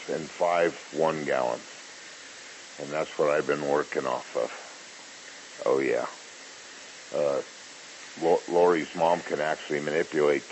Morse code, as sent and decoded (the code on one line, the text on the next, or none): none